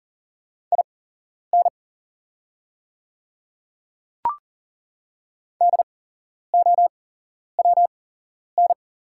Morse code, heard as a keyed tone, about 40 wpm, spaced as for 11 wpm, 700 Hz. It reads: IN DOWN
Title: 8x Character Spacing - Top 100 Words - 40wpm